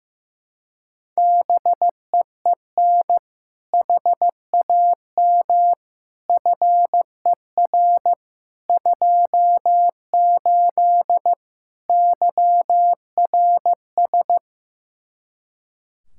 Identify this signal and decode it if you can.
Morse code, heard as a keyed tone, about 15 wpm, 700 Hz. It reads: BEEN HAM FER 28 YRS
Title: Practice QSO - Exchange 3 - More Info - 15wpm